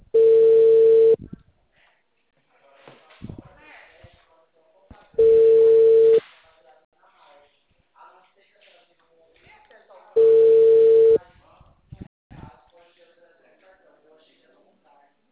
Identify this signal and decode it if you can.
Ringback tone (China, 450 Hz), with speech.
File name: out- -2021